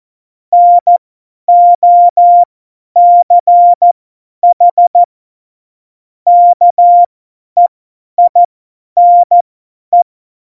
Morse code, heard as a keyed tone, about 14 wpm, spaced as for 10 wpm, 700 Hz. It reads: NOCH KEINE